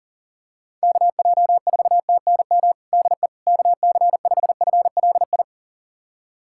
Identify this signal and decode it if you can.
Morse code, heard as a keyed tone, about 37 wpm, 700 Hz. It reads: KJ4TNM DE KC5FLI
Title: USA Callsign DE Callsign - 6 Characters - 40wpm